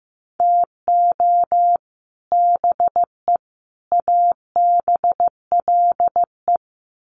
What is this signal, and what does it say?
Morse code, heard as a keyed tone, about 15 wpm, 700 Hz. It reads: TO BE ABLE
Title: Sets of 3 Words 15wpm